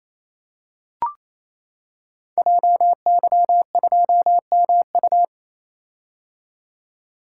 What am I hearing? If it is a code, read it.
Morse code, heard as a keyed tone, about 27 wpm, 700 Hz. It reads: JY2MU
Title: Code Groups - 5 Letter:Numbers - 28wpm